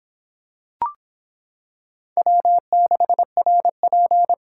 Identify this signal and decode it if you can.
Morse code, heard as a keyed tone, about 25 wpm, 700 Hz. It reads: W6RP